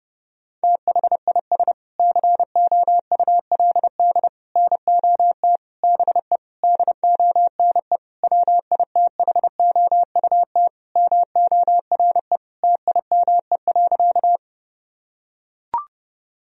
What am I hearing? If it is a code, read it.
Morse code, heard as a keyed tone, about 32 wpm, 700 Hz. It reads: THIS COULD NOT BE DONE WITHOUT MORE TIME.
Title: Sentences from Top 1000 Words - Review Part 3 - 30wpm